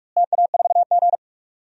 Morse code, heard as a keyed tone, about 46 wpm, 700 Hz. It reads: TA4G
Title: DXCC-100 Call Signs - 4 Characters - Repeated 2x - 45wpm